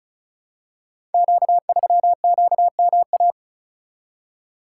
Morse code, heard as a keyed tone, about 36 wpm, 700 Hz. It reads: Q3QMA
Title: Code Groups - 5 Letter:Numbers - 35wpm